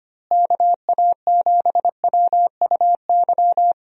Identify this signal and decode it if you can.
Morse code, heard as a keyed tone, about 25 wpm, 700 Hz. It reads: KA7WUY
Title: USA Call Signs - 6 Characters - Repeated 2x - 25wpm